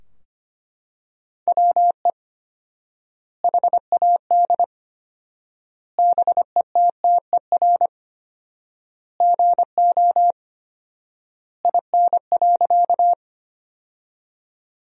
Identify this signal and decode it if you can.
Morse code, heard as a keyed tone, about 25 wpm, 700 Hz. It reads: WE HAD BETTER GO IN.
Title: 4x Word-Spacing - Sentences from Top 400 Words - 25wpm